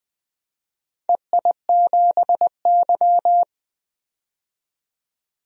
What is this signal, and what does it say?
Morse code, heard as a keyed tone, about 20 wpm, 700 Hz. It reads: EI7Y
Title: DXCC-100 Call Signs - 4 Characters - 20wpm